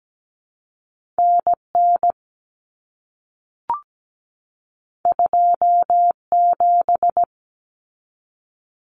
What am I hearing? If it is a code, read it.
Morse code, heard as a keyed tone, about 17 wpm, 700 Hz. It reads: NN 27
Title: Common Cut Numbers - 2-Digit 17wpm